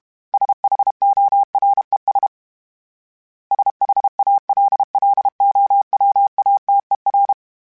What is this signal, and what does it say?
Morse code, heard as a keyed tone, about 32 wpm, 800 Hz. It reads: SHORES SHALLOWATER
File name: fromSMs / FromSMs mp3s / Cities@30